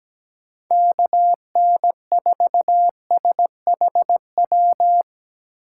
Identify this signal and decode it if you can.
Morse code, heard as a keyed tone, about 19 wpm, 700 Hz. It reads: KN4SHW
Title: Speed Racing - USA 2x3 Call Signs 17wpm